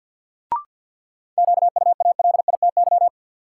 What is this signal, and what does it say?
Morse code, heard as a keyed tone, about 53 wpm, 700 Hz. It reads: QUALITY
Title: Sentences from Top 1500 Words - New Word Drill Alt - 50wpm